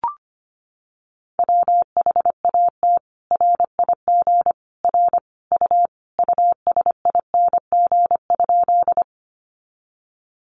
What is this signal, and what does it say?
Morse code, heard as a keyed tone, about 25 wpm, 700 Hz. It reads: WHAT RIG R U USING?
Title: Common QSO Phrases 25wpm